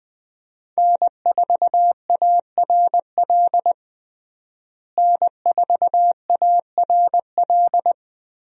Morse code, heard as a keyed tone, about 21 wpm, 700 Hz. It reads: N4ARL N4ARL
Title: USA Call Signs - 5 Characters - Repeated 2x - 20wpm